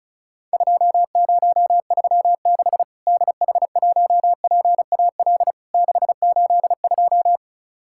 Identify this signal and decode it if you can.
Morse code, heard as a keyed tone, about 36 wpm, 700 Hz. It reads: 2036 DH1PAL 682
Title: Worked All Europe DX Contest - QTC Exchange - 35wpm